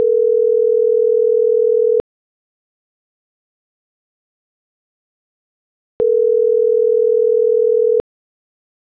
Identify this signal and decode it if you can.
Ringback tone (North America), 440 + 480 Hz.
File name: CENTRAL MAERSK / out isabel.bravo - .70563